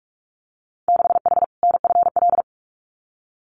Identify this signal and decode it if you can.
Morse code, heard as a keyed tone, about 46 wpm, 700 Hz. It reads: 6H NFL
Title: Winter Field Day Exchanges Encore 45wpm